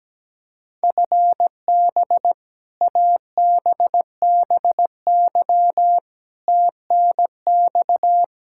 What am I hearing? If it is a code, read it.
Morse code, heard as a keyed tone, about 17 wpm, 700 Hz. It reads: FB ABBY TNX
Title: Practice QSO - Exchange 3 - More Info - 17wpm